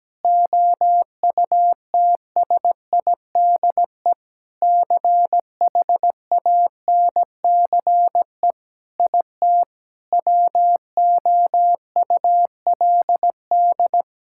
Morse code, heard as a keyed tone, about 17 wpm, 700 Hz. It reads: OUTSIDE CHANCE IT WOULD
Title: Sentences from Top 1000 Words - Review Part 6 - 17wpm